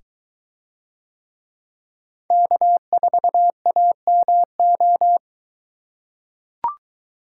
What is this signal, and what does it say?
Morse code, heard as a keyed tone, about 23 wpm, 700 Hz. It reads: K4AMO